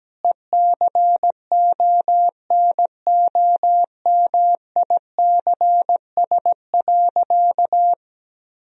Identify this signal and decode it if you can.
Morse code, heard as a keyed tone, about 17 wpm, 700 Hz. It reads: ECONOMICS.
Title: Sentences from Top 1000 Words - Review Part 3 - 17wpm